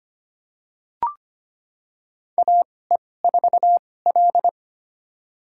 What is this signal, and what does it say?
Morse code, heard as a keyed tone, about 25 wpm, spaced as for 18 wpm, 700 Hz. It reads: AE4L
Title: USA Call Signs - 4 Characters - 2x Character Spacing - 25wpm